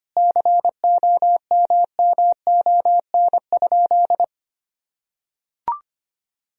Morse code, heard as a keyed tone, about 25 wpm, 700 Hz. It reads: COMMON?
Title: Sentences from Top 1000 Words - Review Part 3 - 25wpm